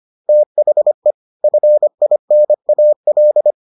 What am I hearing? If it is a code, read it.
Morse code, heard as a keyed tone, about 25 wpm, 600 Hz. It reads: THE FINAL